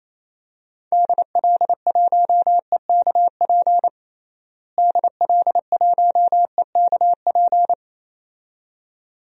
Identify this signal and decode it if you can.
Morse code, heard as a keyed tone, about 27 wpm, 700 Hz. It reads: DL1EKP DL1EKP
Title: DXCC-100 Call Signs - 6 Characters - Repeated 2x - 28wpm